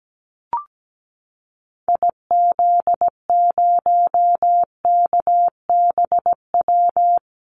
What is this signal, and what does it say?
Morse code, heard as a keyed tone, about 17 wpm, 700 Hz. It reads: IZ0KBW
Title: Call Signs Repeated 2x - 17wpm